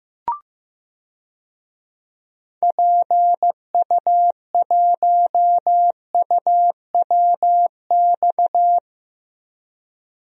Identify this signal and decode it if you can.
Morse code, heard as a keyed tone, about 15 wpm, 700 Hz. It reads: PU1UWX
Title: DXCC-100 Call Signs - 6 Characters - 15wpm